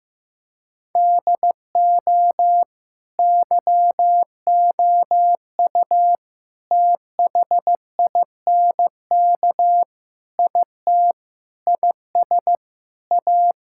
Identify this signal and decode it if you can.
Morse code, heard as a keyed tone, about 16 wpm, 700 Hz. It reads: DO YOU THINK IT IS A
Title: Sentences from Top 1000 Words - Review Part 6 - 15wpm